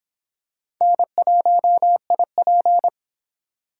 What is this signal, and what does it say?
Morse code, heard as a keyed tone, about 26 wpm, 700 Hz. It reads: N1IP